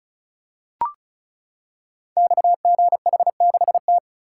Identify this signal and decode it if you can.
Morse code, heard as a keyed tone, about 36 wpm, 690 Hz. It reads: XGH6T
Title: Code Groups - 5 Letter:Numbers - 35wpm